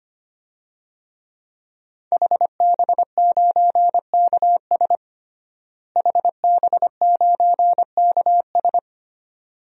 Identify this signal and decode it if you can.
Morse code, heard as a keyed tone, about 25 wpm, 700 Hz. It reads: HB9KS HB9KS
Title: DXCC-100 Call Signs - 5 Characters - Repeated 2x - 25wpm